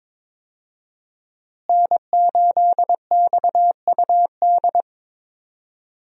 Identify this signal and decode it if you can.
Morse code, heard as a keyed tone, about 22 wpm, 700 Hz. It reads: N8XUD
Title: USA Call Signs - 5 Characters - Repeated 2x - 22wpm